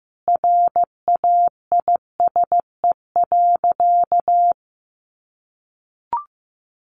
Morse code, heard as a keyed tone, about 15 wpm, 700 Hz. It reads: RAISE.
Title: Easy Phrases 15wpm